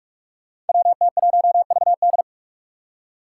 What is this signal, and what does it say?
Morse code, heard as a keyed tone, about 46 wpm, 700 Hz. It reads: WT1VD
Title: Five Challenging Letter-Number Combinations 45wpm